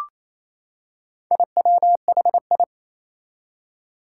Morse code, heard as a keyed tone, about 28 wpm, 700 Hz. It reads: IWHI
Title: Four Challenging Letter-Number Combinations 28wpm